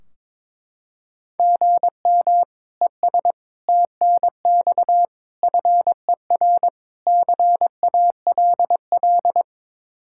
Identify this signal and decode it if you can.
Morse code, heard as a keyed tone, about 22 wpm, 700 Hz. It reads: GM ES TNX FER CALL
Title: Practice QSO - Exchange 1 - Beginning - 22wpm